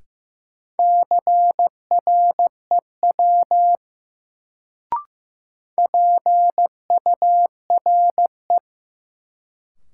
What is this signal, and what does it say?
Morse code, heard as a keyed tone, about 16 wpm, 700 Hz. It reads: CREW PURE
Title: Sentences from Top 1500 Words - New Word Drill - 15wpm